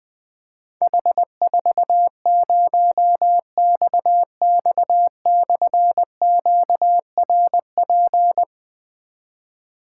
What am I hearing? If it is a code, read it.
Morse code, heard as a keyed tone, about 20 wpm, 700 Hz. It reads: H40XX/QRP
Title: Difficult Call Signs 20wpm